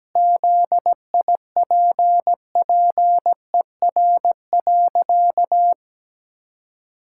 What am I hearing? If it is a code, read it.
Morse code, heard as a keyed tone, about 17 wpm, 700 Hz. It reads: ZIPPER.
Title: Pangram Challenge v2 - 17wpm